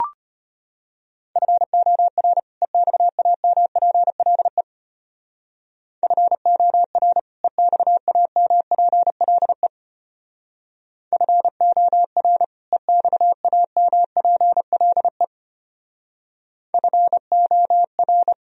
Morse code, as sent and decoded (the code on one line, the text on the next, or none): ..-. --- .-. / . -..- .- -- .--. .-.. / ..-. --- .-. / . -..- .- -- .--. .-.. / ..-. --- .-. / . -..- .- -- .--. .-.. / ..-. --- .-.
FOR EXAMPLE FOR EXAMPLE FOR EXAMPLE FOR